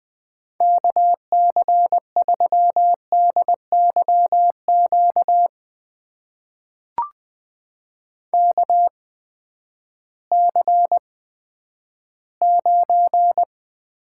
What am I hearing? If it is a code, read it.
Morse code, heard as a keyed tone, about 20 wpm, spaced as for 11 wpm, 700 Hz. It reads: KC3DYQ KC9
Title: USA Call Signs - 6 Characters - 8x Character Spacing - 20wpm